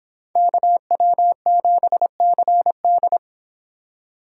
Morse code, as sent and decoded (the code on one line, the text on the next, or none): -.- .-- --... -.-. -..
KW7CD